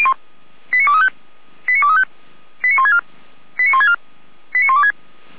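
Five-tone selective call (ZVEI), heard as quick runs of five tones, tones of about 80 ms. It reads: 89235 89235 89164 89164 89126
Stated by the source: ZVEI five-tone call received on an SDR.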